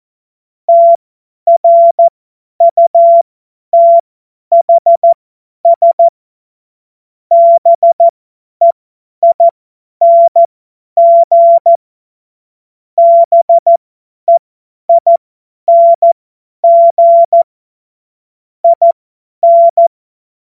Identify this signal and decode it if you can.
Morse code, heard as a keyed tone, about 14 wpm, spaced as for 10 wpm, 680 Hz. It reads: TRUTHS BEING BEING IN
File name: Morsekurs / Witze / Witz 14. Dezember 2021, 50 F70